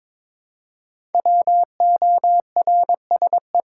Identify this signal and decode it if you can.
Morse code, heard as a keyed tone, about 22 wpm, 700 Hz. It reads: WORSE
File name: 5 Letter Words - First 3 Letters Given - 22wpm